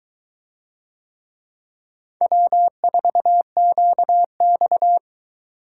Morse code, heard as a keyed tone, about 22 wpm, 700 Hz. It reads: W4QX